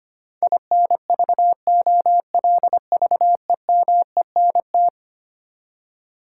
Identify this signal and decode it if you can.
Morse code, heard as a keyed tone, about 25 wpm, 700 Hz. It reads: INVOLVEMENT
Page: morse-code-ninja-practice-25wpm